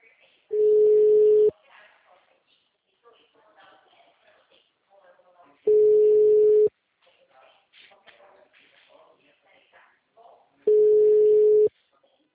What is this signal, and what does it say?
Ringback tone (Europe, 425 Hz), with speech.